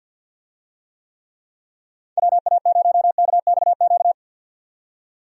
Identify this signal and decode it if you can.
Morse code, heard as a keyed tone, about 46 wpm, 700 Hz. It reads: WA0KXQ
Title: USA Call Signs - 6 Characters - 50wpm